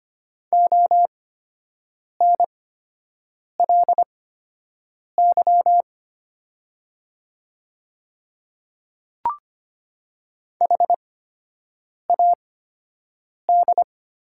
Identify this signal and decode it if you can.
Morse code, heard as a keyed tone, about 25 wpm, spaced as for 7 wpm, 700 Hz. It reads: ONLY HAD
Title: 8x Character Spacing - Top 200 Words - 25wpm